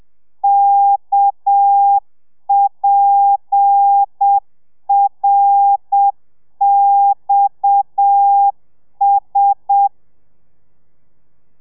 Morse code, heard as a keyed tone, about 7 wpm, 800 Hz. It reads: KPRXS
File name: telegrafia / i2tfj / Lez7 30